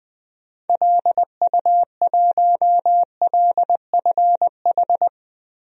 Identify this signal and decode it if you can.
Morse code, heard as a keyed tone, about 21 wpm, 700 Hz. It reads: LU1LFH
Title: DXCC-100 Call Signs - 6 Characters - Repeated 3x - 20wpm